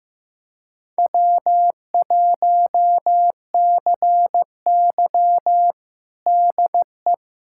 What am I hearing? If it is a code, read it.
Morse code, heard as a keyed tone, about 15 wpm, 700 Hz. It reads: W1CY DE
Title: USA Callsign DE Callsign - 4 Characters - 15wpm